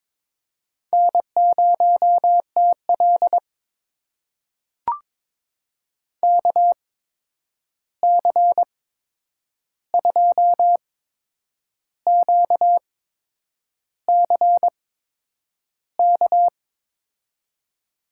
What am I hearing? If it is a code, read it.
Morse code, heard as a keyed tone, about 22 wpm, spaced as for 8 wpm, 700 Hz. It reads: N0TL KC2QCK
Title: USA Call Signs - 8x Character Spacing - 22wpm